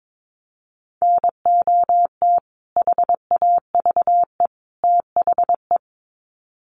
Morse code, heard as a keyed tone, about 22 wpm, 700 Hz. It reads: NOT HAVE THE
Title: Sets of 3 Words 22wpm